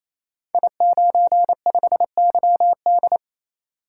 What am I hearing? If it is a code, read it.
Morse code, heard as a keyed tone, about 28 wpm, 700 Hz. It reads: I95YD
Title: Code Groups - 5 Letter:Numbers - 28wpm